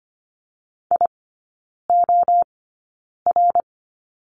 Morse code, heard as a keyed tone, about 25 wpm, spaced as for 9 wpm, 700 Hz. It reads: IOR
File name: Lesson 15e - Number 5 - Three Characters 25wpm